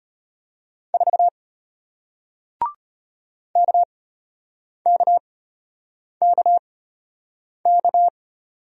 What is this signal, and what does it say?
Morse code, heard as a keyed tone, about 33 wpm, spaced as for 8 wpm, 700 Hz. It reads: 4 KKKK